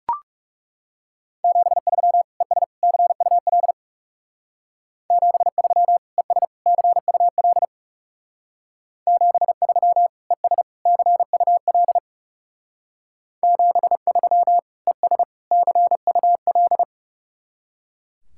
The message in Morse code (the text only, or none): 73 ES CUL 73 ES CUL 73 ES CUL 73 ES CUL